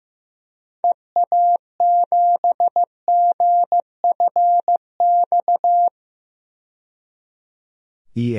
Morse code, heard as a keyed tone, about 15 wpm, 700 Hz. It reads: EA7GFX
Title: DXCC-100 Call Signs - 6 Characters - 15wpm